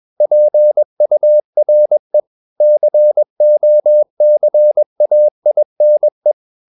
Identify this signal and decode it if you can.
Morse code, heard as a keyed tone, about 21 wpm, 600 Hz. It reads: PURE COCAINE
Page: CW med Gnister